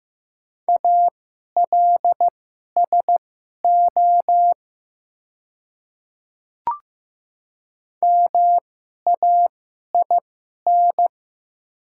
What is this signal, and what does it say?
Morse code, heard as a keyed tone, about 15 wpm, spaced as for 11 wpm, 700 Hz. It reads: ALSO MAIN
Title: 2x Character Spacing - Top 300 Words - 15wpm